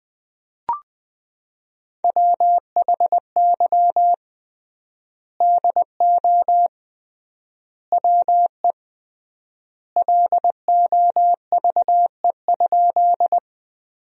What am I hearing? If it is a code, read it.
Morse code, heard as a keyed tone, about 20 wpm, 700 Hz. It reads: WHY DO WE LOVE?
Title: 3x Word-Spacing - Sentences from Top 400 Words - 20wpm